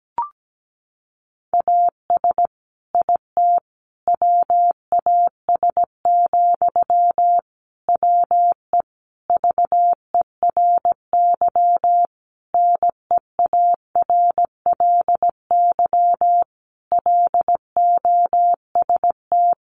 Morse code, heard as a keyed tone, about 17 wpm, 700 Hz. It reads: AS IT WAS, WE VERY NEARLY LOST
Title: Sentences from Top 1400 Words 17wpm